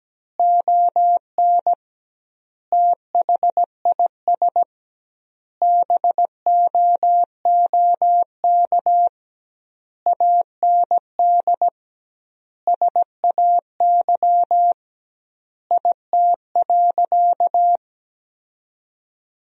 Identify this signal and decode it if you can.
Morse code, heard as a keyed tone, about 17 wpm, 700 Hz. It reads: ON THIS BOOK AND SAY IT.
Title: 2x Word-Spacing - Sentences from Top 400 Words - 17wpm